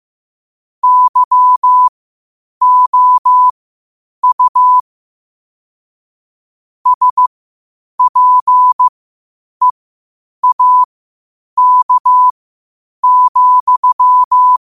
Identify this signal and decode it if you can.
Morse code, heard as a keyed tone, about 15 wpm, spaced as for 8 wpm, 1000 Hz. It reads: YOU SPEAK,
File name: qotdfiles / 8wpm